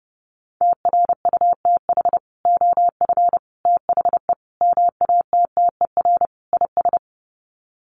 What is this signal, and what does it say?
Morse code, heard as a keyed tone, about 30 wpm, 700 Hz. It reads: TRUTH OF THE MATTER IS
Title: Sets of 5 Words Encore2 30wpm